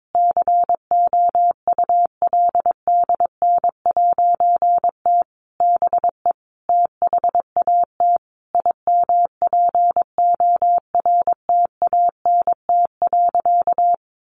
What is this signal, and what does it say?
Morse code, heard as a keyed tone, about 22 wpm, 700 Hz. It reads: COULDN'T BE THAT IMPORTANT.
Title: A Sound Of Thunder 22wpm